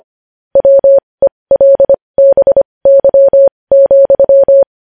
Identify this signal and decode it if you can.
Morse code, heard as a keyed tone, about 25 wpm, spaced as for 20 wpm, 550 Hz. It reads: WELBY,
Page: news headlines in morse code at 20 wpm